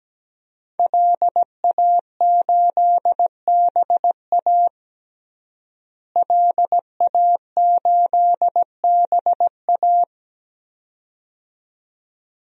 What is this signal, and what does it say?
Morse code, heard as a keyed tone, about 17 wpm, 700 Hz. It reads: LA8BA LA8BA
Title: DXCC-100 Call Signs - 5 Characters - Repeated 2x - 17wpm